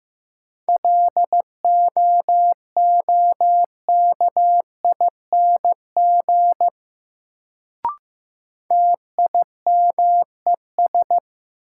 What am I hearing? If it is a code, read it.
Morse code, heard as a keyed tone, about 15 wpm, 700 Hz. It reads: LOOKING TIMES
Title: Sentences from Top 1000 Words - Review Part 6 - New Word Drill - 15wpm